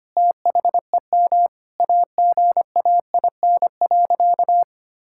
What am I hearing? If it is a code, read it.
Morse code, heard as a keyed tone, about 25 wpm, 700 Hz. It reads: THEM AGAIN.
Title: Sentences from Top 1000 Words - Review Part 6 - 25wpm